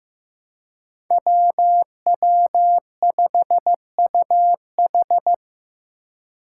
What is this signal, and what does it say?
Morse code, heard as a keyed tone, about 15 wpm, 700 Hz. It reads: WW5UH
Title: Five Challenging Letter-Number Combinations 15wpm